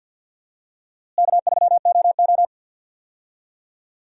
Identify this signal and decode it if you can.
Morse code, heard as a keyed tone, about 51 wpm, 700 Hz. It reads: K3OO
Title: USA Call Signs - 4 Characters - Repeated 3x - 50wpm